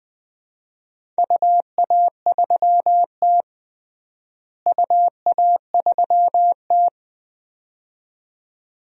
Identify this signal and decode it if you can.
Morse code, heard as a keyed tone, about 20 wpm, 700 Hz. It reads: UA3T UA3T
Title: DXCC-100 Call Signs - 4 Characters - Repeated 2x - 20wpm